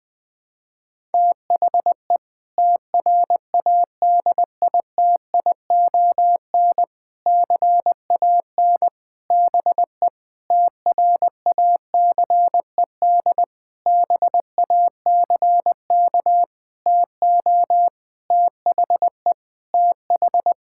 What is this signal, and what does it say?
Morse code, heard as a keyed tone, about 20 wpm, 700 Hz. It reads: THE TRADITION CAN BE TRACED BACK TO THE TH
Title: Sentences with 6-Word Phrases - 20wpm